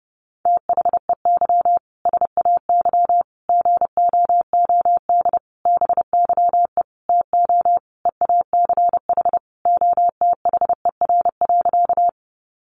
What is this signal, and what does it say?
Morse code, heard as a keyed tone, about 30 wpm, 700 Hz. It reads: THEY SAY GOOD BYE TO EACH OTHER.
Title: Easy Phrases 30wpm